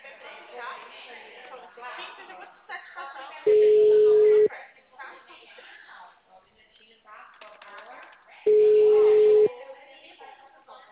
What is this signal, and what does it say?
Ringback tone (Europe, 425 Hz), with speech.
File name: out- -2044- - - .23034